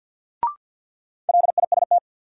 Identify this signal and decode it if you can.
Morse code, heard as a keyed tone, about 53 wpm, 700 Hz. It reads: LIST
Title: Sentences from Top 400 Words - New Word Drill - 50wpm